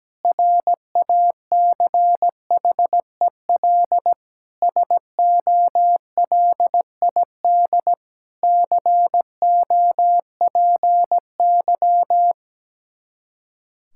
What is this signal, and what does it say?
Morse code, heard as a keyed tone, about 17 wpm, 700 Hz. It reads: RACHEL SOLID COPY
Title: K1USN SST QSO Practice 17wpm